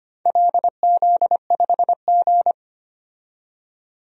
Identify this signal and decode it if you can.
Morse code, heard as a keyed tone, about 25 wpm, 700 Hz. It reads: LZ5G
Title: DXCC-100 Call Signs - 4 Characters - 25wpm